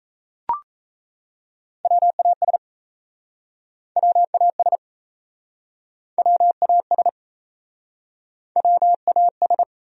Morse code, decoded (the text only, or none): WAS WAS WAS WAS